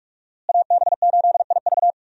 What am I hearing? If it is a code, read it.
Morse code, heard as a keyed tone, about 46 wpm, 700 Hz. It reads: AB8IV
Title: USA Call Signs - 5 Characters - Repeated 3x - 45wpm